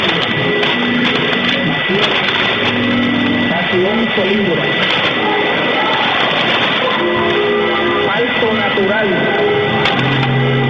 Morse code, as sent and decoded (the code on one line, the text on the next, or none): none